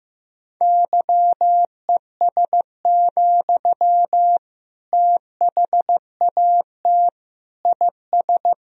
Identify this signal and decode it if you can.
Morse code, heard as a keyed tone, about 15 wpm, 700 Hz. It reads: YES, THAT IS
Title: Sentences from Top 1000 Words - Review Part 3 - 15wpm